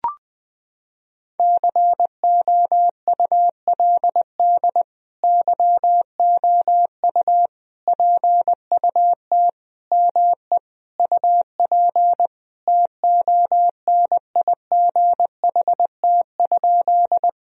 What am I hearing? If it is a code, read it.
Morse code, heard as a keyed tone, about 21 wpm, 700 Hz. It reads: COULD YOU PUT ME UP TONIGHT?
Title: Sentences from Top 1000 Words - Review Part 3 - 20wpm